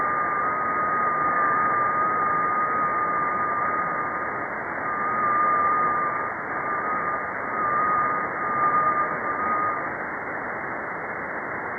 FT8.
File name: contests / 2019-june=vhf-k5tr- -0101 0-2